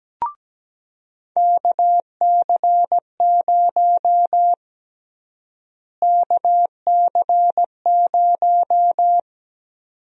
Morse code, heard as a keyed tone, about 17 wpm, 700 Hz. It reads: KC0 KC0
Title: USA Call Sign Prefixes - Repeated 2x - 17wpm